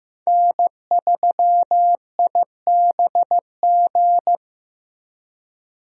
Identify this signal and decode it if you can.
Morse code, heard as a keyed tone, about 15 wpm, 700 Hz. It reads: N3IBG
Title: USA Call Signs - 5 Characters - 15wpm